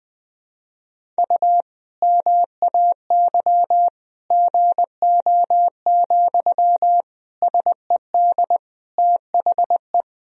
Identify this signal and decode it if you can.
Morse code, heard as a keyed tone, about 20 wpm, 700 Hz. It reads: U MAY GO, SED THE
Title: Sentences from Top 400 Words and Abbreviations 20wpm